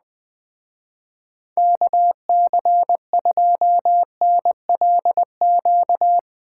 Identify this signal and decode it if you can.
Morse code, heard as a keyed tone, about 20 wpm, 700 Hz. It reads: KC2NLQ